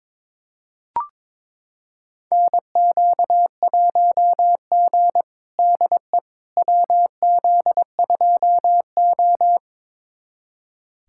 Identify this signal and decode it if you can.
Morse code, heard as a keyed tone, about 22 wpm, 700 Hz. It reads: NQ1G DE WZ2O
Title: USA Callsign DE Callsign - 4 Characters - 22wpm